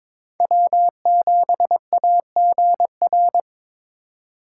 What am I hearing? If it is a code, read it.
Morse code, heard as a keyed tone, about 22 wpm, 700 Hz. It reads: W7AGR
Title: USA Call Signs - 5 Characters - 4x Character Spacing - 22wpm